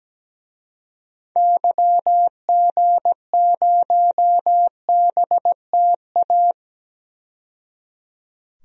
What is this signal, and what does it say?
Morse code, heard as a keyed tone, about 17 wpm, 700 Hz. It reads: YG0BTA
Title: DXCC-100 Call Signs - 6 Characters - 17wpm